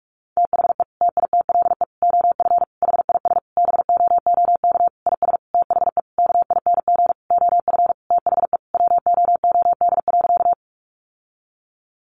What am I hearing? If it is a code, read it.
Morse code, heard as a keyed tone, about 44 wpm, 700 Hz. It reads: THE TITLE OF HIS BOOK IS THE KING OF THE WOOD.
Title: Sentences from Top 1200 Words 45wpm